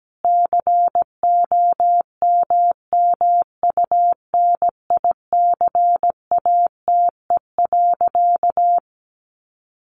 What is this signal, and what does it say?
Morse code, heard as a keyed tone, about 17 wpm, 700 Hz. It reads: COMMUNICATE.
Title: Sentences from Top 1300 Words 17wpm